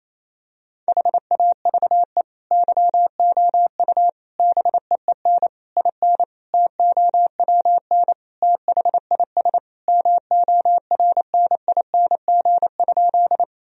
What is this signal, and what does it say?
Morse code, heard as a keyed tone, about 28 wpm, 700 Hz. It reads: HAVE YOU BEEN IN TOWN THIS MORNING?
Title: Sentences from Top 1000 Words - Review Part 2 - 28wpm